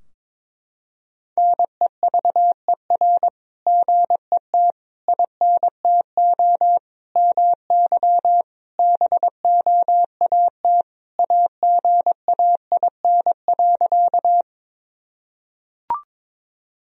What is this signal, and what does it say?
Morse code, heard as a keyed tone, about 21 wpm, 700 Hz. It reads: NEVER GET INTO MY BOAT AGAIN.
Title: Sentences from Top 1000 Words - Review Part 3 - 22wpm